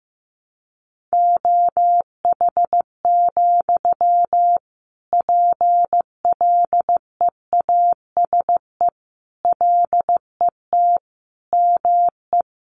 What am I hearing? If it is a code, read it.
Morse code, heard as a keyed tone, about 15 wpm, 700 Hz. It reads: OH, PLEASE LET ME
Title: Sentences from Top 1500 Words 15wpm